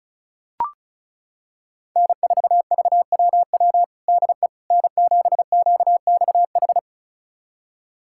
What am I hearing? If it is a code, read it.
Morse code, heard as a keyed tone, about 36 wpm, 700 Hz. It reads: N4VWW DE N7QXH
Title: USA Callsign DE Callsign - 5 Characters - 35wpm